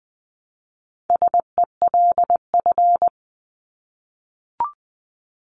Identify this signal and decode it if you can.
Morse code, heard as a keyed tone, about 20 wpm, 700 Hz. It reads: SELF
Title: Top 500 Words 20wpm